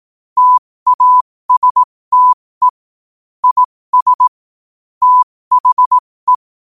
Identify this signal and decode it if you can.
Morse code, heard as a keyed tone, about 18 wpm, spaced as for 15 wpm, 1000 Hz. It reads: TASTE IS THE
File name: qotdfiles / 15wpm